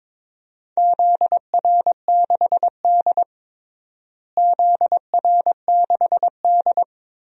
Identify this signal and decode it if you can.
Morse code, heard as a keyed tone, about 22 wpm, 700 Hz. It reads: ZR6D ZR6D
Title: DXCC-100 Call Signs - 4 Characters - Repeated 2x - 22wpm